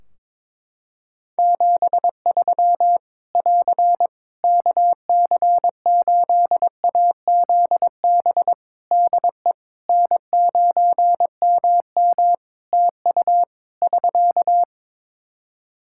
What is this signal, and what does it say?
Morse code, heard as a keyed tone, about 21 wpm, 700 Hz. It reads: 73 <AR> KC8AZB DE N9MM TU <SK>
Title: K1USN SST QSO Practice 22wpm